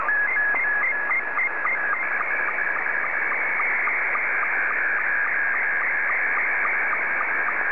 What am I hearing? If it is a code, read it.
SSTV (slow-scan television), Scottie 2.
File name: sstv / S2-VK3HJV